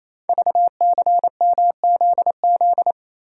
Morse code, heard as a keyed tone, about 28 wpm, 700 Hz. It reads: VCMZZ
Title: Code Groups - 5 Letters - 28wpm